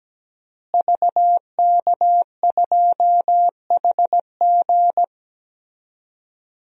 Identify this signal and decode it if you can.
Morse code, heard as a keyed tone, about 17 wpm, 700 Hz. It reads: VK2HG
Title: DXCC-100 Call Signs - 5 Characters - Repeated 3x - 17wpm